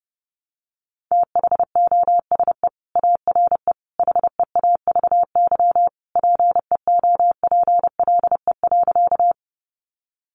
Morse code, heard as a keyed tone, about 30 wpm, 700 Hz. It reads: THOSE ARE HEAVY PEOPLE.
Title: Sentences from Top 1100 Words 30wpm